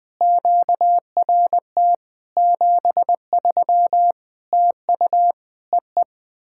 Morse code, heard as a keyed tone, about 21 wpm, 700 Hz. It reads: QRT 73 TU EE
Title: POTA QSO Practice - 20wpm